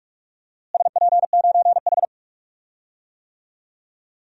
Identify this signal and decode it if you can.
Morse code, heard as a keyed tone, about 46 wpm, 700 Hz. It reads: SP9H
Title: DXCC-100 Call Signs - 4 Characters - Repeated 3x - 45wpm